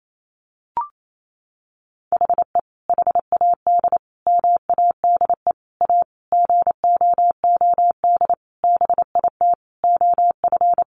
Morse code, heard as a keyed tone, about 28 wpm, 700 Hz. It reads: HE HAD MADE A GOOD BIT OF
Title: Sentences from Top 900 Words 28wpm